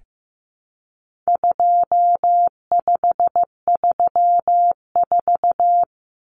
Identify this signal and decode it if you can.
Morse code, heard as a keyed tone, about 15 wpm, 700 Hz. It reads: 2534
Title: Numbers - 4-Digit 15wpm